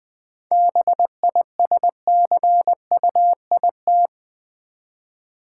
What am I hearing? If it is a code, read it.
Morse code, heard as a keyed tone, about 20 wpm, 690 Hz. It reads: BISCUIT
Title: 7 Letter Words - First 3 Letters Given - 20wpm